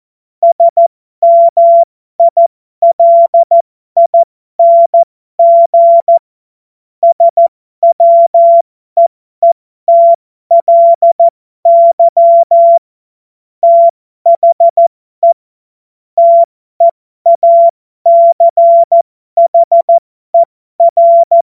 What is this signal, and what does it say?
Morse code, heard as a keyed tone, about 14 wpm, spaced as for 12 wpm, 680 Hz. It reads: SMILING SWEETLY THE TEACHER
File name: Morsekurs / Witze